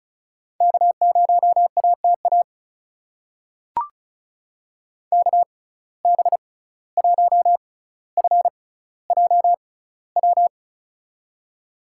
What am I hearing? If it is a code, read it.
Morse code, heard as a keyed tone, about 36 wpm, spaced as for 17 wpm, 700 Hz. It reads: K0ATA KB1FJW
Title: USA Call Signs - 6x Character Spacing - 35wpm